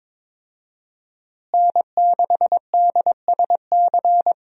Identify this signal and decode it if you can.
Morse code, heard as a keyed tone, about 22 wpm, 700 Hz. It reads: N6DSC
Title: USA Call Signs - 5 Characters - Repeated 2x - 22wpm